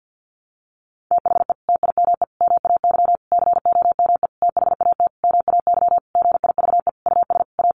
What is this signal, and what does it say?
Morse code, heard as a keyed tone, about 50 wpm, 700 Hz. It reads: THE TIME MAY COME THAT MAY GIVE US A
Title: Sentences from Top 1100 Words 50wpm